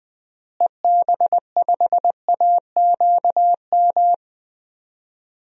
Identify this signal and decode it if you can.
Morse code, heard as a keyed tone, about 20 wpm, 700 Hz. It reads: EB5AQM
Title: DXCC-100 Call Signs - 6 Characters - Repeated 3x - 20wpm